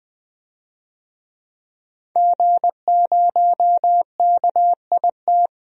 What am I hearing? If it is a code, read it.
Morse code, heard as a keyed tone, about 20 wpm, 700 Hz. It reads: G0KIT
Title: DXCC-100 Call Signs - 5 Characters - Repeated 2x - 20wpm